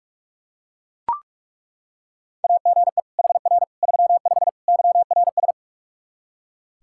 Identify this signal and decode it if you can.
Morse code, heard as a keyed tone, about 41 wpm, 700 Hz. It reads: AGE HR 35 YRS
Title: Practice QSO - Exchange 3 - More Info - 45wpm